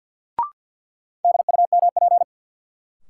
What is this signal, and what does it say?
Morse code, heard as a keyed tone, about 47 wpm, 700 Hz. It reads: DUMP
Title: Sentences from Top 1300 Words - New Word Drill Alt - 50wpm